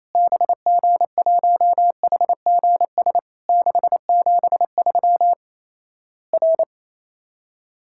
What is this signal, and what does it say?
Morse code, heard as a keyed tone, about 28 wpm, 700 Hz. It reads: BG1HGS 673 R
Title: Worked All Europe DX Contest - QTC Exchange - 28wpm